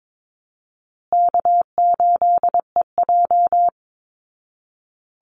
Morse code, heard as a keyed tone, about 22 wpm, 700 Hz. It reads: K8EJ
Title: Call Signs Repeated 2x - 22wpm